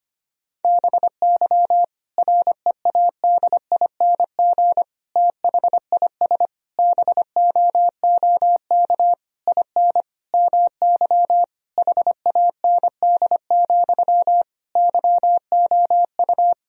Morse code, heard as a keyed tone, about 25 wpm, 700 Hz. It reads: BY READING THIS BOOK IN MY HAND, YOU
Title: Sentences from Top 1000 Words - Review Part 3 - 25wpm